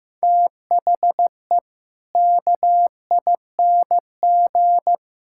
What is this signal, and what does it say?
Morse code, heard as a keyed tone, about 15 wpm, 700 Hz. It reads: THE KING
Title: Sentences from Top 1000 Words - Review Part 5 - 15wpm